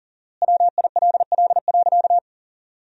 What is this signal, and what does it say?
Morse code, heard as a keyed tone, about 40 wpm, 700 Hz. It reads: WILL.
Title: 4x Word-Spacing - Sentences from Top 100 Words - 40wpm